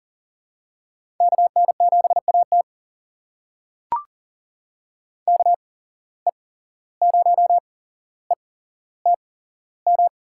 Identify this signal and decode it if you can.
Morse code, heard as a keyed tone, about 40 wpm, spaced as for 16 wpm, 700 Hz. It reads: KN7AT KE0ETM
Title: USA Call Signs - 8x Character Spacing - 40wpm